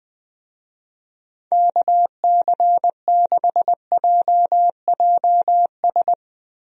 Morse code, heard as a keyed tone, about 19 wpm, 700 Hz. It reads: KC6JJS
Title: USA Call Signs - 6 Characters - Repeated 2x - 20wpm